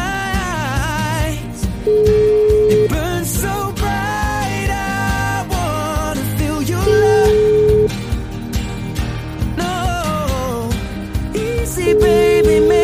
Ringback tone (Europe, 425 Hz), with noise.